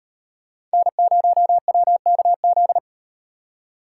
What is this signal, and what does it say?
Morse code, heard as a keyed tone, about 38 wpm, 700 Hz. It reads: N0WKZ